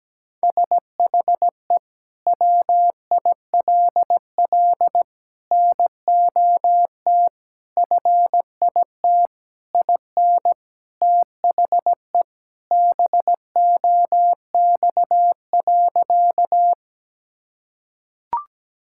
Morse code, heard as a keyed tone, about 17 wpm, 700 Hz. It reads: SHE WILL NOT FIT IN THE BOX.
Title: Sentences from Top 1000 Words - Review Part 5 - 17wpm